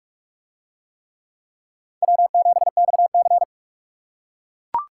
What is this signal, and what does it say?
Morse code, heard as a keyed tone, about 42 wpm, 700 Hz. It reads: W7XC